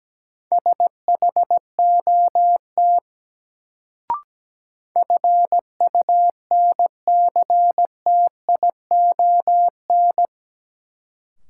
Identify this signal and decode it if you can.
Morse code, heard as a keyed tone, about 17 wpm, 700 Hz. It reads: SHOT FUNCTION
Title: Sentences from Top 1100 Words - New Word Drill - 17wpm